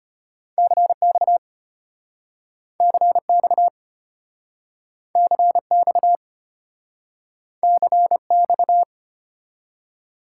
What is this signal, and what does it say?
Morse code, heard as a keyed tone, about 33 wpm, 700 Hz. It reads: CX CX CX CX